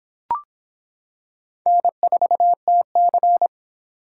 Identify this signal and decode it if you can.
Morse code, heard as a keyed tone, about 25 wpm, 700 Hz. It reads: N4TC